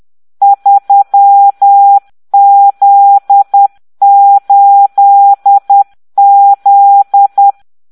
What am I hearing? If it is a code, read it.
Morse code, heard as a keyed tone, about 10 wpm, 800 Hz. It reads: 3Z8Z